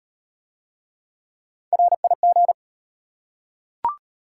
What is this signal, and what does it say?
Morse code, heard as a keyed tone, about 35 wpm, 690 Hz. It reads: RIG